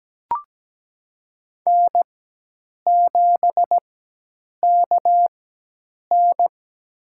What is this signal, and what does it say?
Morse code, heard as a keyed tone, about 17 wpm, spaced as for 8 wpm, 700 Hz. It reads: N7KN
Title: USA Call Signs - 4 Characters - 4x Character Spacing - 17wpm